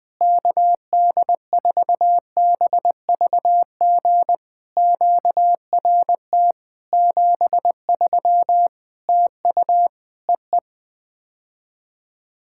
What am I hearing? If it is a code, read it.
Morse code, heard as a keyed tone, about 20 wpm, 700 Hz. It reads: KD4BVG QRT 73 TU EE
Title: POTA QSO Practice - 20wpm